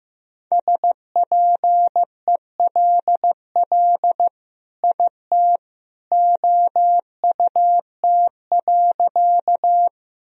Morse code, heard as a keyed tone, about 15 wpm, 700 Hz. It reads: SPELL IT OUT.
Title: Sentences from Top 1000 Words - Review Part 1 - 15wpm